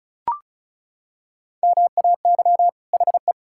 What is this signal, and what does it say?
Morse code, heard as a keyed tone, about 34 wpm, 700 Hz. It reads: MAY HE
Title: Sentences from Top 1000 Words - Review Part 6 - 35wpm